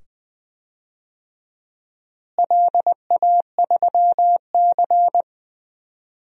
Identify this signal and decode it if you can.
Morse code, heard as a keyed tone, about 20 wpm, 700 Hz. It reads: LA3C
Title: DXCC-100 Call Signs - 4 Characters - Repeated 2x - 20wpm